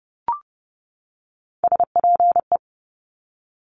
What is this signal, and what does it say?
Morse code, heard as a keyed tone, about 30 wpm, 700 Hz. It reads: SPE
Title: Top Three Letter Combinations in Words 30wpm